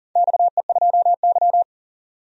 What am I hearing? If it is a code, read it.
Morse code, heard as a keyed tone, about 40 wpm, 700 Hz. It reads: XE2Y
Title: DXCC-100 Call Signs - 4 Characters - 40wpm